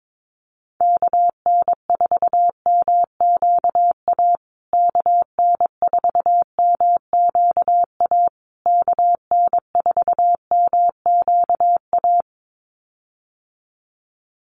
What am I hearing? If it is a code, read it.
Morse code, heard as a keyed tone, about 22 wpm, 700 Hz. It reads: KN4MQA KN4MQA KN4MQA
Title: USA Call Signs Repeated 3x 22wpm